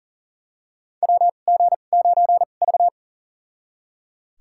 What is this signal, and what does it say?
Morse code, heard as a keyed tone, about 40 wpm, spaced as for 29 wpm, 700 Hz. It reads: WG9V
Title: USA Call Signs - 4 Characters - 2x Character Spacing - 40wpm